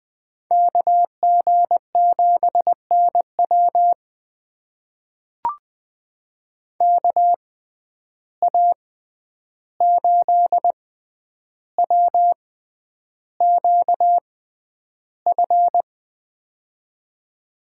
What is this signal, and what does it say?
Morse code, heard as a keyed tone, about 20 wpm, spaced as for 10 wpm, 700 Hz. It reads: KG7NW KA8WQF
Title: USA Call Signs - 6x Character Spacing - 20wpm